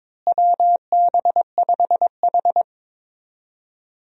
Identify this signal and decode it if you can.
Morse code, heard as a keyed tone, about 22 wpm, 700 Hz. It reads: WB5H
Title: USA Call Signs - 4 Characters - Repeated 3x - 22wpm